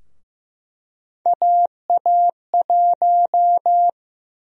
Morse code, heard as a keyed tone, about 15 wpm, 700 Hz. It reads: AA1
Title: USA Call Sign Prefixes - Repeated 2x - 15wpm